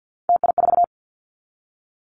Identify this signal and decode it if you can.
Morse code, heard as a keyed tone, about 50 wpm, 700 Hz. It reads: TI4
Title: Callsign Prefixes for Top 100 DXCC - 50wpm